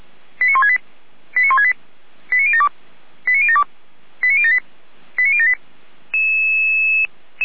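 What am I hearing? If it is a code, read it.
Five-tone selective call (ZVEI), heard as quick runs of five tones, tones of about 70 ms. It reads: 89168 89168 89072 89072 89078 89078
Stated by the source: Five-tone ZVEI selective call received on an SDR.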